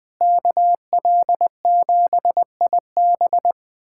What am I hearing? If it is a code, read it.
Morse code, heard as a keyed tone, about 21 wpm, 700 Hz. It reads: KL7IB
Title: USA Callsign DE Callsign 20wpm